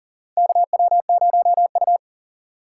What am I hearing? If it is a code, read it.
Morse code, heard as a keyed tone, about 40 wpm, 700 Hz. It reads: KW0U
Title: USA Call Signs - 4 Characters - 3x Character Spacing - 40wpm